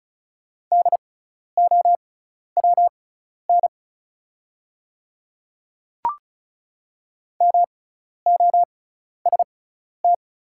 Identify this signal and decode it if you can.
Morse code, heard as a keyed tone, about 36 wpm, spaced as for 12 wpm, 700 Hz. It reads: DOWN MOST
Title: 6x Character Spacing - Top 100 Words - 35wpm